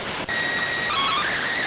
SSTV (slow-scan television), PD 290.